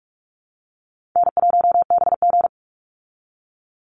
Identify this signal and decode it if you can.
Morse code, heard as a keyed tone, about 44 wpm, 700 Hz. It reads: N1BG
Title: Easy Call Signs Repeated 3x - 45wpm